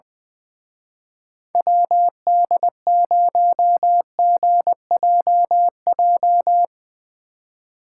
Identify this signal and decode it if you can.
Morse code, heard as a keyed tone, about 20 wpm, 700 Hz. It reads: WD0GJJ
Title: Speed Racing - USA 2x3 Call Signs 20wpm